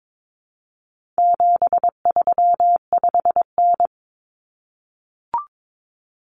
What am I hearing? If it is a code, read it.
Morse code, heard as a keyed tone, about 22 wpm, 700 Hz. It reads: 735N
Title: Common Cut Numbers - 4-Digit 22wpm